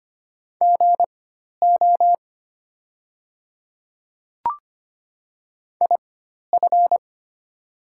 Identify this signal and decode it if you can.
Morse code, heard as a keyed tone, about 25 wpm, spaced as for 12 wpm, 700 Hz. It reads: GO IF
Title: 4x Character Spacing - Top 100 Words - 25wpm